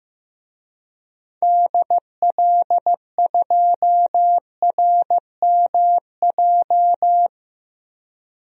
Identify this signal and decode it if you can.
Morse code, heard as a keyed tone, about 15 wpm, 700 Hz. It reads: DL2RMJ
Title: DXCC-100 Call Signs - 6 Characters - 15wpm